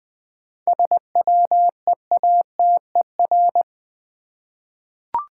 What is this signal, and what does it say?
Morse code, heard as a keyed tone, about 20 wpm, 700 Hz. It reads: SWEATER
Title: 7 Letter Words - First 3 Letters Given - 20wpm